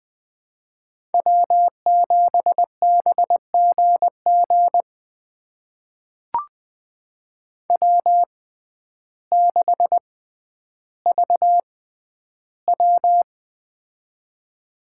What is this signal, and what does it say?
Morse code, heard as a keyed tone, about 20 wpm, spaced as for 11 wpm, 700 Hz. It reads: W7BGG W6VW
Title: USA Call Signs - 6x Character Spacing - 20wpm